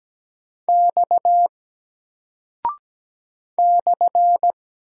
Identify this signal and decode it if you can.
Morse code, heard as a keyed tone, about 17 wpm, 700 Hz. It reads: X /
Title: Single Character (Letter:Number:Punctuation:Prosign) 17wpm